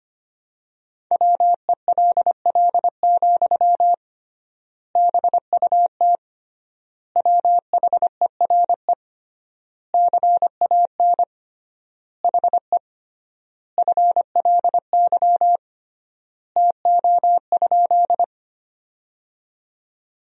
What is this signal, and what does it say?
Morse code, heard as a keyed tone, about 25 wpm, 700 Hz. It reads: WELL, BUT WHERE CAN HE FLY TO?
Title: 3x Word-Spacing - Sentences from Top 500 Words - 25wpm